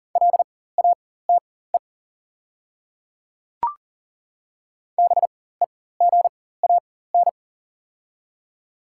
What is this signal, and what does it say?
Morse code, heard as a keyed tone, about 37 wpm, spaced as for 18 wpm, 690 Hz. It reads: LATE BEGAN
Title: 4x Character Spacing - Top 300 Words - 40wpm